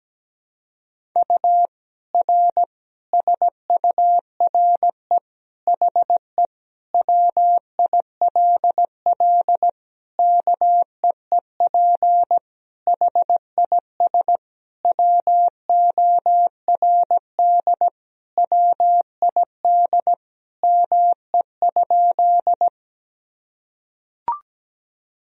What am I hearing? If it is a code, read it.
Morse code, heard as a keyed tone, about 17 wpm, 700 Hz. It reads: U R SURE HE WILL KEEP HIS WORD WID ME?
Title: Sentences from Top 300 Words and Abbreviations 17wpm